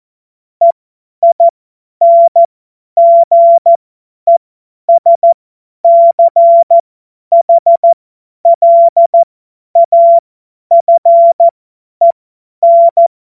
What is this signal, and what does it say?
Morse code, heard as a keyed tone, about 14 wpm, spaced as for 10 wpm, 680 Hz. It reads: EINGESCHLAFEN
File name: Morsekurs / Witze / Witz 26. Oktober 2021, 50 F70